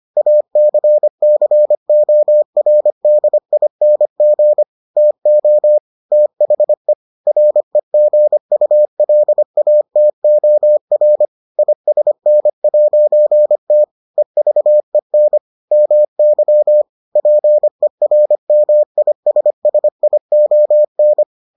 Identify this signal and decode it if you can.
Morse code, heard as a keyed tone, about 25 wpm, 600 Hz. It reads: ACCORDING TO THE REGULATOR ISN'T EVEN MY PERMISSION